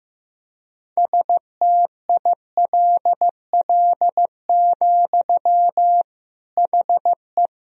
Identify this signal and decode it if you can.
Morse code, heard as a keyed tone, about 15 wpm, 700 Hz. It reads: STILL, HE
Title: Sentences from Top 1000 Words - Review Part 1 - 15wpm